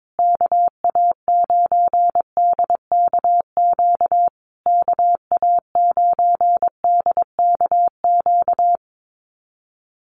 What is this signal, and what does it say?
Morse code, heard as a keyed tone, about 22 wpm, 700 Hz. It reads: KA9DKQ KA9DKQ
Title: USA Call Signs Repeated 3x 22wpm